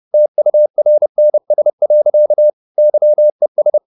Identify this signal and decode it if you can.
Morse code, heard as a keyed tone, about 30 wpm, 600 Hz. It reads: TURNS. YES